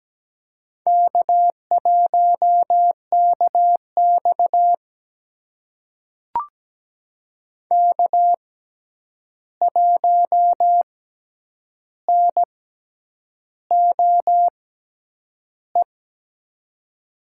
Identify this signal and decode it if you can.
Morse code, heard as a keyed tone, about 17 wpm, spaced as for 8 wpm, 700 Hz. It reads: K1KX K1NOE
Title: USA Call Signs - 6x Character Spacing - 17wpm